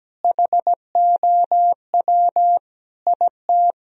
Morse code, heard as a keyed tone, about 17 wpm, 700 Hz. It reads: HOW IT
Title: Sentences from Top 1000 Words - Review Part 3 - 17wpm